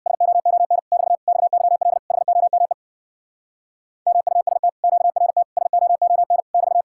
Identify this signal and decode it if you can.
Morse code, heard as a keyed tone, about 64 wpm, 700 Hz. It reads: SOON <BT> XYL HOME MUST QRT SOON <BT>